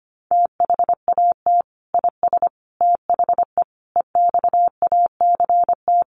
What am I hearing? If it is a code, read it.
Morse code, heard as a keyed tone, about 25 wpm, 700 Hz. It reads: THAT IS THE EXACT
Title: Sentences from Top 1300 Words 25wpm